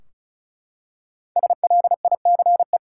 Morse code, heard as a keyed tone, about 36 wpm, 700 Hz. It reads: SLICE
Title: Five Letter Words - 35wpm